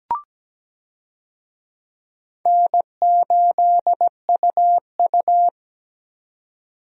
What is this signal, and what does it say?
Morse code, heard as a keyed tone, about 17 wpm, 700 Hz. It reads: N8UU
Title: USA Call Signs - 4 Characters - 17wpm